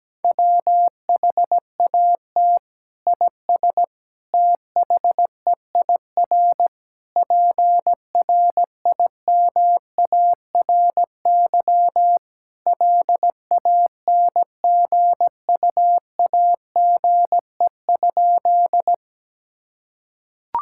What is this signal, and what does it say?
Morse code, heard as a keyed tone, about 17 wpm, 700 Hz. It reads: WHAT IS THEIR PRIMARY LANGUAGE?
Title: Sentences from Top 1000 Words - Review Part 5 - 17wpm